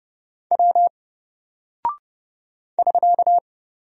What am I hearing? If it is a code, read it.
Morse code, heard as a keyed tone, about 30 wpm, 700 Hz. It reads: W <SK>
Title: Single Character (Letter:Number:Punctuation:Prosign) 30wpm